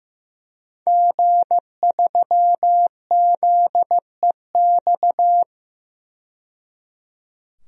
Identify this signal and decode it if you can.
Morse code, heard as a keyed tone, about 15 wpm, 700 Hz. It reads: G3ZEX
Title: DXCC-100 Call Signs - 5 Characters - 15wpm